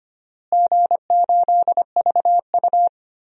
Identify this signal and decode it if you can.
Morse code, heard as a keyed tone, about 25 wpm, 700 Hz. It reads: G8VU
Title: DXCC-100 Call Signs - 4 Characters - Repeated 3x - 25wpm